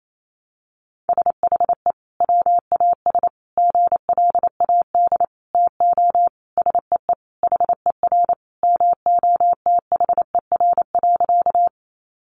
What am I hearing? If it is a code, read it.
Morse code, heard as a keyed tone, about 28 wpm, 700 Hz. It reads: SHE WAS GLAD TO SEE HER MOTHER.
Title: Sentences from Top 800 Words 28wpm